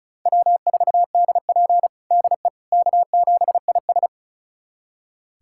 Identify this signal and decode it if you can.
Morse code, heard as a keyed tone, about 36 wpm, 700 Hz. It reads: W4DP DE K7IS
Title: USA Callsign DE Callsign - 4 Characters - 35wpm